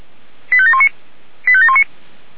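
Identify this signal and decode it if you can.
Five-tone selective call (ZVEI), heard as quick runs of five tones, tones of about 70 ms. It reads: 85719 85719
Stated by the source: ZVEI five-tone selective call received on an SDR.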